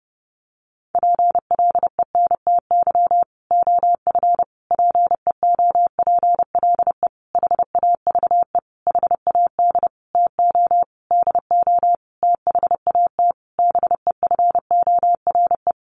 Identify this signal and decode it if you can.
Morse code, heard as a keyed tone, about 30 wpm, 700 Hz. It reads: PLENTY OF PEOPLE HAVE HAD TO DO THAT BEFORE
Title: Sentences from Top 1300 Words 30wpm